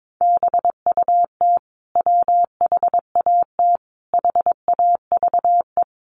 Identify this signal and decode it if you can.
Morse code, heard as a keyed tone, about 22 wpm, 700 Hz. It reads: BUT WHAT HAVE
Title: Sentences from Top 100 Words 22wpm